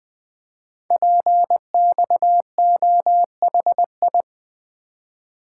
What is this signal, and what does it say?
Morse code, heard as a keyed tone, about 20 wpm, 700 Hz. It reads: PXOHI
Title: Code Groups - 5 Letters - 20wpm